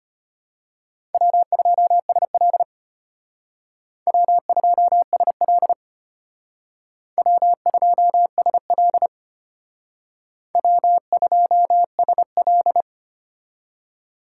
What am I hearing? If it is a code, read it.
Morse code, heard as a keyed tone, about 34 wpm, 700 Hz. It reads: W2SL W2SL W2SL W2SL